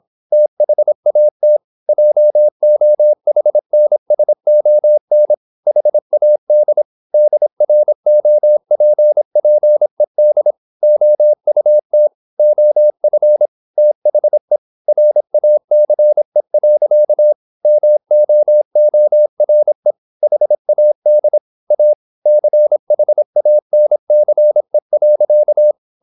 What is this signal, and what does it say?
Morse code, heard as a keyed tone, about 26 wpm, 600 Hz. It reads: THAT JOHNSON HAD DROPPED OUT OF THE RACE. MOORE HAD A CHANCE.
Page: Never 26wpm | CW med Gnister